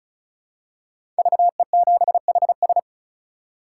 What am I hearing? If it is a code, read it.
Morse code, heard as a keyed tone, about 34 wpm, 700 Hz. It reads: VE7HS
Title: K1USN SST QSO Practice - 35wpm